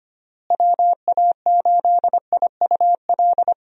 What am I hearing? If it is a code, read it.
Morse code, heard as a keyed tone, about 25 wpm, 700 Hz. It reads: WA8IUL
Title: USA Call Signs - 6 Characters - Repeated 3x - 25wpm